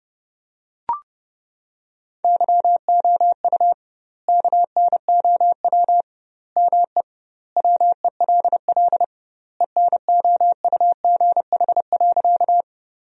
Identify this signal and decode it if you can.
Morse code, heard as a keyed tone, about 30 wpm, 700 Hz. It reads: YOU KNOW ME WELL ENOUGH.
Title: 2x Word-Spacing - Sentences from Top 400 Words - 30wpm